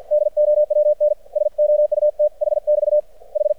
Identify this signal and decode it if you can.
Morse code, heard as a keyed tone, about 50 wpm, 600 Hz. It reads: DOWN SOUTH <BT> H
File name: archived / CRSnet